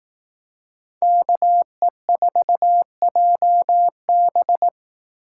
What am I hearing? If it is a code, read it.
Morse code, heard as a keyed tone, about 18 wpm, 700 Hz. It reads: KE4JB
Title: Speed Racing - USA 2x2 Call Signs 15wpm